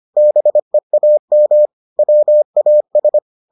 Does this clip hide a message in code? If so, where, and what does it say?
Yes: Morse code from 0.2 to 3.2 s, heard as a keyed tone, about 25 wpm, 600 Hz. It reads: BEAM WAS